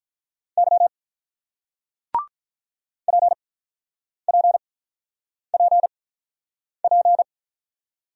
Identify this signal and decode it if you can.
Morse code, heard as a keyed tone, about 48 wpm, spaced as for 9 wpm, 700 Hz. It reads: Y PPPP